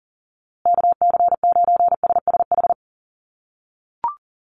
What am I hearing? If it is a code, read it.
Morse code, heard as a keyed tone, about 40 wpm, 700 Hz. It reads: KC9SSH
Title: USA Call Signs Repeated 2x 40wpm